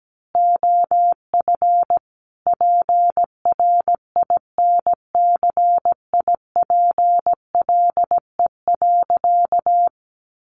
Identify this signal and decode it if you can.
Morse code, heard as a keyed tone, about 17 wpm, 700 Hz. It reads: OF PRINCIPLE.
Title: Sentences from Top 1400 Words 17wpm